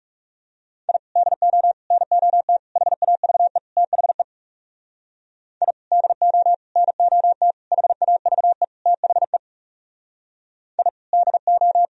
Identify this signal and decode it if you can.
Morse code, heard as a keyed tone, about 38 wpm, 700 Hz. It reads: I DO NOT HAVE THE I DO NOT HAVE THE I DO